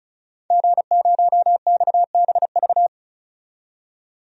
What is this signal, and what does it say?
Morse code, heard as a keyed tone, about 36 wpm, 700 Hz. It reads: G0XBV
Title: DXCC-100 Call Signs - 5 Characters - 35wpm